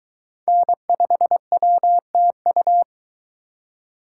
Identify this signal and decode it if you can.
Morse code, heard as a keyed tone, about 21 wpm, 700 Hz. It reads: N5WTU